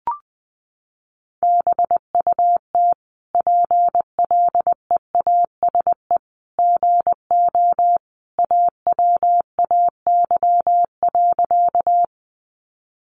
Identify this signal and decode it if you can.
Morse code, heard as a keyed tone, about 20 wpm, 700 Hz. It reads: BUT PLEASE GO AWAY.
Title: Sentences from Top 1500 Words 20wpm